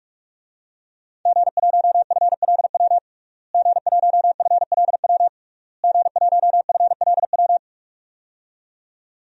Morse code, heard as a keyed tone, about 46 wpm, 700 Hz. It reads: G1FLW G1FLW G1FLW
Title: DXCC-100 Call Signs - 5 Characters - Repeated 3x - 45wpm